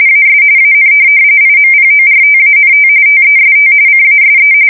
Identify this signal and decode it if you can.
RTTY, about 45.45 baud. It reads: AS AN IMPORTANT TEST CASE
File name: fsk170 / USS-Missouri